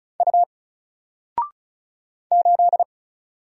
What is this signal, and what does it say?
Morse code, heard as a keyed tone, about 36 wpm, 700 Hz. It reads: U 8
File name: Single-Letter-Number-35wpm